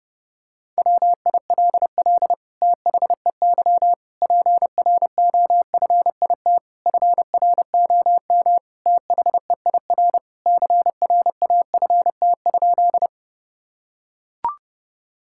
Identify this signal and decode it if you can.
Morse code, heard as a keyed tone, about 30 wpm, 700 Hz. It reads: WILL THEY PROFIT FROM THEIR CRAFT?
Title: Sentences from Top 1000 Words - Review Part 6 - 30wpm